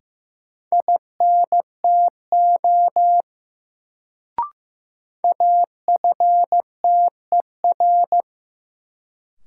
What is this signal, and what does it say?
Morse code, heard as a keyed tone, about 15 wpm, 700 Hz. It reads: INTO AFTER
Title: Sets of 2 Words - New Word Drill - 15wpm